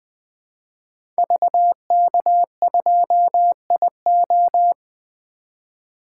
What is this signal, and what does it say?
Morse code, heard as a keyed tone, about 20 wpm, 700 Hz. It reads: VK2IO
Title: 2023 POTA Activator Call Signs - 20wpm